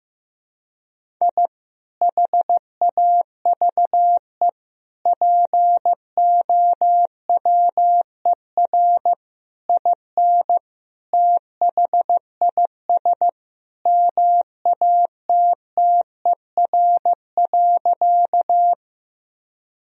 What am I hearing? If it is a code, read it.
Morse code, heard as a keyed tone, about 15 wpm, 700 Hz. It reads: I HAVE POWER IN THIS MATTER.
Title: Sentences from Top 1000 Words - Review Part 2 - 15wpm